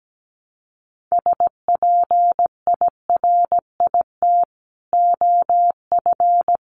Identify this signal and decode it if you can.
Morse code, heard as a keyed tone, about 17 wpm, 700 Hz. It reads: SPIRIT OF
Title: Sentences from Top 1400 Words 17wpm